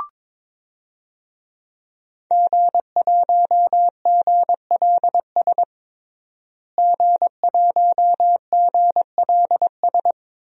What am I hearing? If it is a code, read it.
Morse code, heard as a keyed tone, about 22 wpm, 700 Hz. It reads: G1GLS G1GLS
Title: DXCC-100 Call Signs - 5 Characters - Repeated 2x - 22wpm